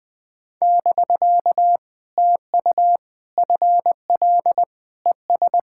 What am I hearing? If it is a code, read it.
Morse code, heard as a keyed tone, about 20 wpm, 700 Hz. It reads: <BK> TU FL ES
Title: POTA QSO Practice - 20wpm